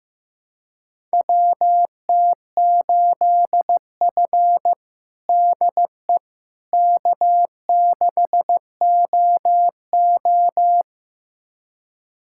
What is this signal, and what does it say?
Morse code, heard as a keyed tone, about 15 wpm, 700 Hz. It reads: WT8F DE K6OO
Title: USA Callsign DE Callsign - 4 Characters - 15wpm